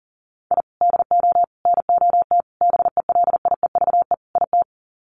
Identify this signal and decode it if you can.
Morse code, heard as a keyed tone, about 40 wpm, 700 Hz. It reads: I DO NOT BELIEVE IT
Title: Sets of 5 Words 40wpm